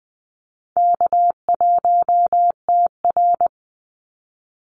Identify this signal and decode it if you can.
Morse code, heard as a keyed tone, about 19 wpm, 700 Hz. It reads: K1TR
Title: Easy Call Signs - 20wpm